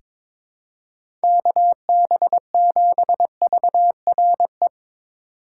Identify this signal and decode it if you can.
Morse code, heard as a keyed tone, about 21 wpm, 700 Hz. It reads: KB7VRE
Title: USA Call Signs - 6 Characters - 6x Character Spacing - 22wpm